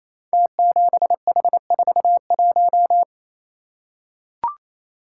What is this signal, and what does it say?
Morse code, heard as a keyed tone, about 27 wpm, 700 Hz. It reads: T7H41
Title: Code Groups - 5 Letter:Numbers - 28wpm